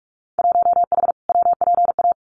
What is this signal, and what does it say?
Morse code, heard as a keyed tone, about 44 wpm, 700 Hz. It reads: 1H WPA
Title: Winter Field Day Exchanges 45wpm